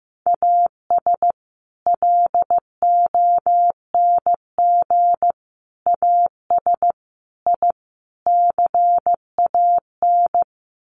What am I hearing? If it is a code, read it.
Morse code, heard as a keyed tone, about 15 wpm, 700 Hz. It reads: AS LONG AS I CAN
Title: Sets of 5 Words Encore2 15wpm